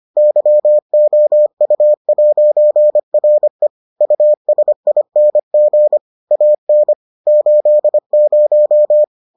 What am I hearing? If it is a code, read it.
Morse code, heard as a keyed tone, about 25 wpm, 600 Hz. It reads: YOU'RE USING AN 80